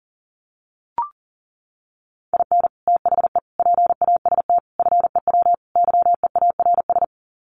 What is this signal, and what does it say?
Morse code, heard as a keyed tone, about 40 wpm, 700 Hz. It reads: IN THE PAST FEW YEARS
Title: Sets of 5 Words 40wpm